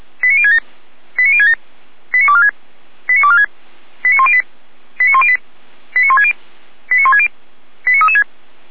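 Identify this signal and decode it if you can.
Five-tone selective call (ZVEI), heard as quick runs of five tones, tones of about 70 ms. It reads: 89067 89067 89246 89246 89108 89108 89160 89160 89336
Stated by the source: Five-tone ZVEI selective call 89099 received on SDR, logged by fire brigade Feuerwehr Orlamünde.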